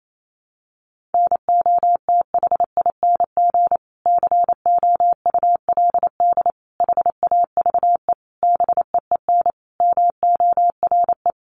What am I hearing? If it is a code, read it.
Morse code, heard as a keyed tone, about 28 wpm, 700 Hz. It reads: NOTHING COULD HAVE BEEN MORE
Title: Sentences from Top 1300 Words 28wpm